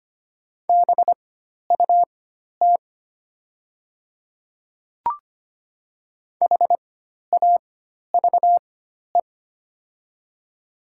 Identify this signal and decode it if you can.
Morse code, heard as a keyed tone, about 25 wpm, spaced as for 12 wpm, 700 Hz. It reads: BUT HAVE
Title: 4x Character Spacing - Top 100 Words - 25wpm